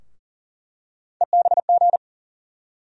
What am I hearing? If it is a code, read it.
Morse code, heard as a keyed tone, about 40 wpm, 700 Hz. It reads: EBG
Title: USA Call Sign Postfixes - 3 Letters 40wpm